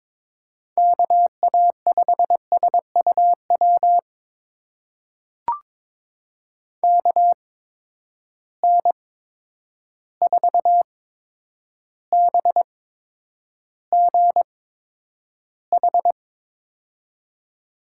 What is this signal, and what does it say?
Morse code, heard as a keyed tone, about 22 wpm, spaced as for 9 wpm, 700 Hz. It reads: KA5SUW KN4BGH
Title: USA Call Signs - 6 Characters - 8x Character Spacing - 22wpm